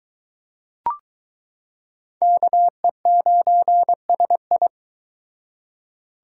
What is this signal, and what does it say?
Morse code, heard as a keyed tone, about 22 wpm, 700 Hz. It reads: KE9SI